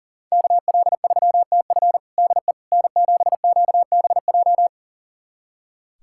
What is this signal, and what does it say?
Morse code, heard as a keyed tone, about 40 wpm, 690 Hz. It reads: KL3TF DE N7QBJ
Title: USA Callsign DE Callsign - 5 Characters - 40wpm